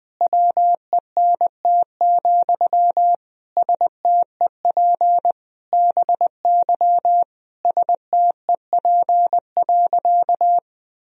Morse code, heard as a keyed tone, about 20 wpm, 700 Hz. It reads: WENT, STEP BY STEP.
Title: Sentences from Top 1000 Words - Review Part 2 - 20wpm